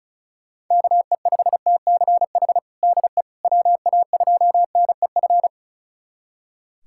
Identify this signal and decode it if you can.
Morse code, heard as a keyed tone, about 34 wpm, 700 Hz. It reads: KE5TCH DE WA2NEF
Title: USA Callsign DE Callsign 35wpm